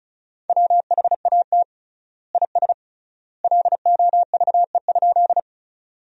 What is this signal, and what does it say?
Morse code, heard as a keyed tone, about 36 wpm, 700 Hz. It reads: WHAT IS LOVE?
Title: 3x Word-Spacing - Sentences from Top 400 Words - 35wpm